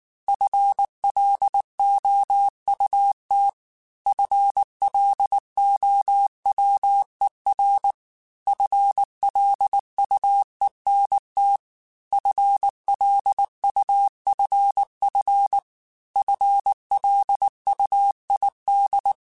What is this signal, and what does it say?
Morse code, heard as a keyed tone, about 19 wpm, 800 Hz. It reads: FLOUT FLOWER FLUENT FLUFF FLUID
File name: FL Runs / FL Runs mp3s / FL Runs @18